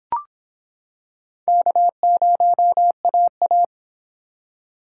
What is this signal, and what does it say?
Morse code, heard as a keyed tone, about 26 wpm, 700 Hz. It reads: K0AA